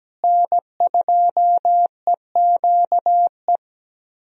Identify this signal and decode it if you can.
Morse code, heard as a keyed tone, about 17 wpm, 700 Hz. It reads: N2EQE
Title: USA Call Signs - 5 Characters - Repeated 2x - 17wpm